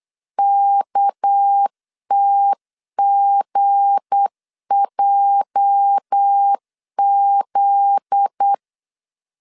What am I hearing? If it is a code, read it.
Morse code, heard as a keyed tone, about 8 wpm, 800 Hz. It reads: KTGJZ
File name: LEZIONI / 9 286 8wpm 3425